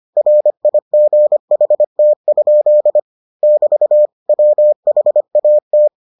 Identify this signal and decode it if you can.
Morse code, heard as a keyed tone, about 25 wpm, 600 Hz. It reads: RIGHT? <BT> WHAT